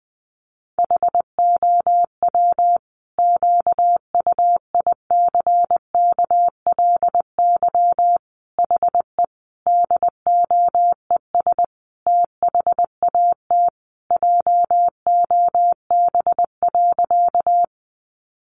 Morse code, heard as a keyed tone, about 19 wpm, 700 Hz. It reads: HOW QUICKLY HE DOES THAT JOB.
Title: Sentences from Top 700 Words 20wpm